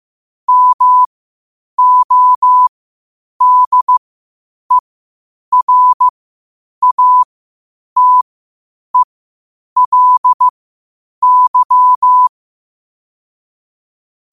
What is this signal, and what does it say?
Morse code, heard as a keyed tone, about 15 wpm, spaced as for 8 wpm, 1000 Hz. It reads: MODERATELY
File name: qotdfiles / 8wpm